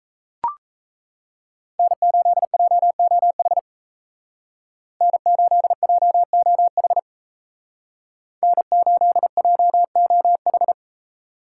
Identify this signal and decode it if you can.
Morse code, heard as a keyed tone, about 36 wpm, 700 Hz. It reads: N8JOH N8JOH N8JOH